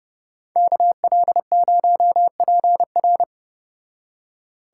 Morse code, heard as a keyed tone, about 31 wpm, 700 Hz. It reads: KL0PR